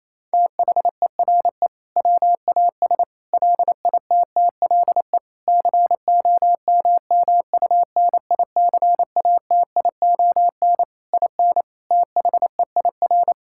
Morse code, heard as a keyed tone, about 27 wpm, 700 Hz. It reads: THERE WAS LITTLE COMMUNICATION IN THEIR
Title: Sentences from Top 1000 Words - Review Part 6 - 28wpm